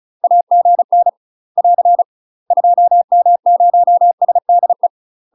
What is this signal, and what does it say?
Morse code, heard as a keyed tone, about 36 wpm, 700 Hz. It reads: AGN <AR> 2M0SDE
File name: UK MASTER QSOs TRAINING / 35WPM / UK MASTER QSOs 0001